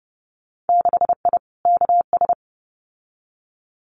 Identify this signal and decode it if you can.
Morse code, heard as a keyed tone, about 30 wpm, 700 Hz. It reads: 6I KS
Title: Winter Field Day Exchanges 30wpm